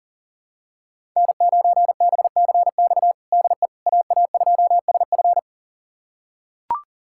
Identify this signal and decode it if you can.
Morse code, heard as a keyed tone, about 40 wpm, 700 Hz. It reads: N9BCX DE AA2SF
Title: USA Callsign DE Callsign - 5 Characters - 40wpm